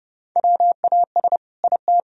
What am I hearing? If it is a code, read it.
Morse code, heard as a keyed tone, about 30 wpm, 700 Hz. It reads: WAS IT
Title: Sentences from Top 1000 Words - Review Part 4 - 30wpm